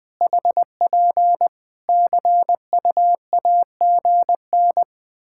Morse code, heard as a keyed tone, about 20 wpm, 700 Hz. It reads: HP CUAGN
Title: Practice QSO - Exchange 4 - Ending - 20wpm